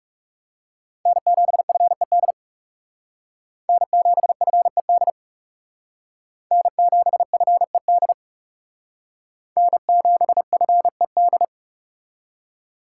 Morse code, heard as a keyed tone, about 40 wpm, 700 Hz. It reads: N7FED N7FED N7FED N7FED